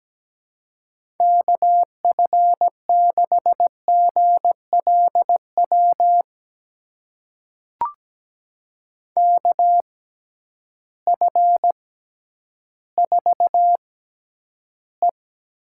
Morse code, heard as a keyed tone, about 17 wpm, spaced as for 10 wpm, 700 Hz. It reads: KF6GLW KF4E
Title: USA Call Signs - 6 Characters - 6x Character Spacing - 17wpm